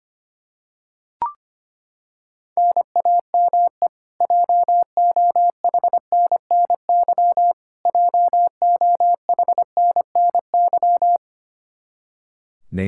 Morse code, heard as a keyed tone, about 25 wpm, 700 Hz. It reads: NAME JOHNNY JOHNNY
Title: Practice QSO - Exchange 1 - Beginning - 25wpm